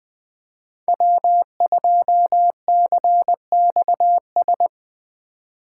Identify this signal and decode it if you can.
Morse code, heard as a keyed tone, about 20 wpm, 700 Hz. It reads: W2CXS
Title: Code Groups - 5 Letter:Numbers - 20wpm